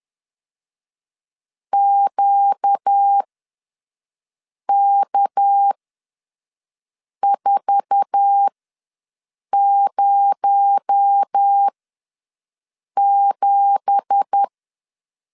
Morse code, heard as a keyed tone, about 10 wpm, spaced as for 5 wpm, 800 Hz. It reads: QK407
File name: LEZIONI / 9 2867 ws 10wpm 4215